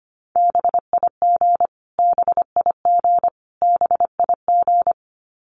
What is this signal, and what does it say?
Morse code, heard as a keyed tone, about 25 wpm, 700 Hz. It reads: BIG BIG BIG
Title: Top 200 Words Repeated 25wpm